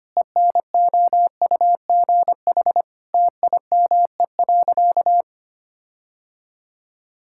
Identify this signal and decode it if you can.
Morse code, heard as a keyed tone, about 25 wpm, 700 Hz. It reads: ENOUGH TIME.
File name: practice / 25wpm / Sentences with 4-Word Phrases - 25wpm